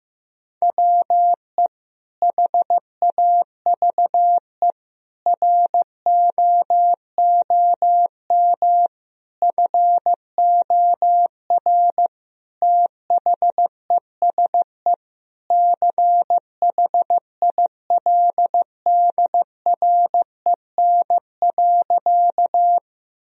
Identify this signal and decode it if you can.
Morse code, heard as a keyed tone, about 15 wpm, 700 Hz. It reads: WE HAVE ROOM FOR THESE CHILDREN.
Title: Sentences from Top 1000 Words - Review Part 1 - 15wpm